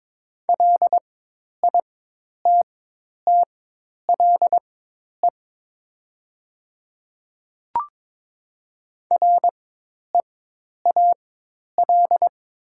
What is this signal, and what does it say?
Morse code, heard as a keyed tone, about 22 wpm, spaced as for 10 wpm, 700 Hz. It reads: LITTLE REAL
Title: 4x Character Spacing - Top 300 Words - 22wpm